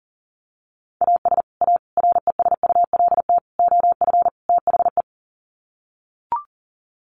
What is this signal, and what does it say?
Morse code, heard as a keyed tone, about 40 wpm, 700 Hz. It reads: AS A RESULT OF THE
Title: Sets of 5 Words 40wpm